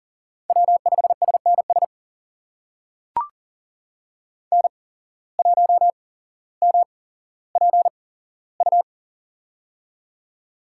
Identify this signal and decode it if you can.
Morse code, heard as a keyed tone, about 40 wpm, spaced as for 17 wpm, 700 Hz. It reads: W5SNS N1MPU
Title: USA Call Signs - 8x Character Spacing - 40wpm